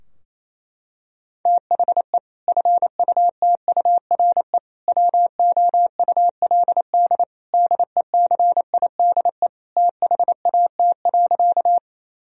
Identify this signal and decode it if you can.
Morse code, heard as a keyed tone, about 27 wpm, 700 Hz. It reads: THE FUTURE WOULD DECIDE THAT.
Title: Sentences from Top 1000 Words - Review Part 3 - 28wpm